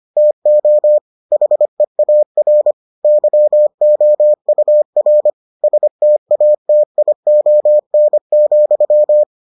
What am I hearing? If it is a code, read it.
Morse code, heard as a keyed tone, about 25 wpm, 600 Hz. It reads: TO HEAR YOUR STATION,